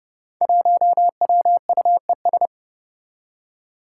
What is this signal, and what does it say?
Morse code, heard as a keyed tone, about 32 wpm, 700 Hz. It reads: 1WUES
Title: Five Challenging Letter-Number Combinations 30wpm